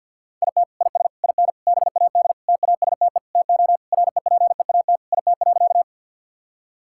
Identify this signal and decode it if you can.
Morse code, heard as a keyed tone, about 50 wpm, 700 Hz. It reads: IT IS IN BAD TASTE TO REPEAT IT.
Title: Sentences from Top 1600 Words 50wpm